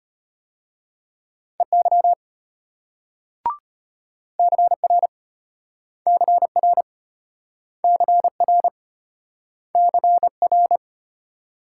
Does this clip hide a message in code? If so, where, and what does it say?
Yes: Morse code from 1.6 to 10.8 s, heard as a keyed tone, about 33 wpm, 700 Hz. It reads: EY CR CR CR CR